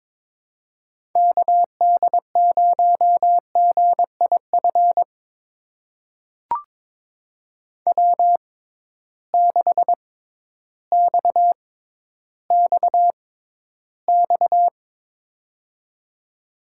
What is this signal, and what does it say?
Morse code, heard as a keyed tone, about 22 wpm, spaced as for 12 wpm, 700 Hz. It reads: KD0GIF W6XXX
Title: USA Call Signs - 6x Character Spacing - 22wpm